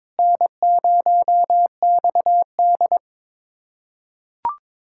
Morse code, heard as a keyed tone, about 22 wpm, 700 Hz. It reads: N0XD
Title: USA Callsign DE Callsign - 4 Characters - 22wpm